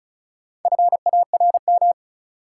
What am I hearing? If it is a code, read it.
Morse code, heard as a keyed tone, about 36 wpm, 700 Hz. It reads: FARM
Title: Sentences from Top 1200 Words - New Word Drill Alt - 35wpm